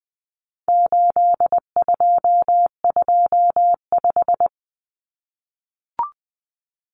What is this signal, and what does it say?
Morse code, heard as a keyed tone, about 20 wpm, 700 Hz. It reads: 8225
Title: Common Cut Numbers - 4-Digit 20wpm